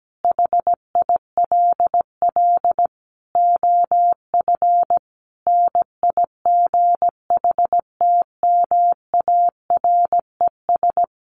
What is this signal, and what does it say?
Morse code, heard as a keyed tone, about 17 wpm, 700 Hz. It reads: HILL OF NIGHTMARES
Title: A Sound Of Thunder 17wpm